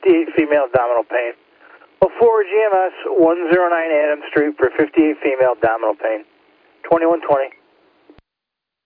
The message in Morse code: none

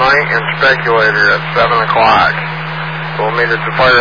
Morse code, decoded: none

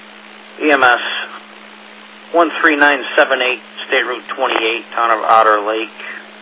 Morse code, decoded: none

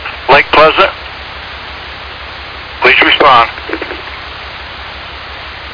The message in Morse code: none

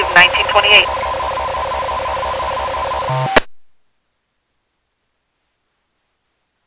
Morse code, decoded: none

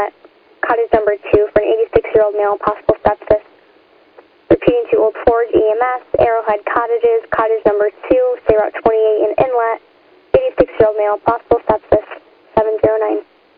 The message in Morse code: none

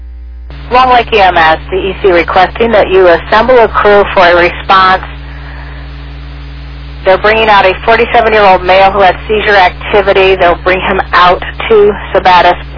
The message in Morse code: none

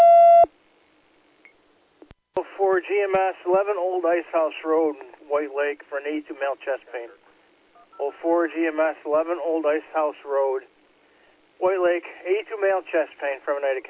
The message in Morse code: none

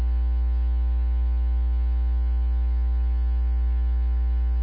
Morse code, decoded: none

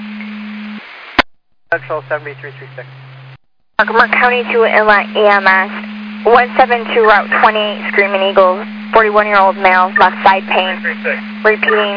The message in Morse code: none

